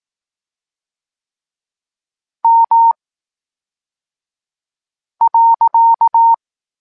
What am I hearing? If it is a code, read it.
Morse code, heard as a keyed tone, about 18 wpm, 920 Hz. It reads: M .
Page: Tekst 23.3 – Ciąg dalszy doskonalenia odbioru poznanych liter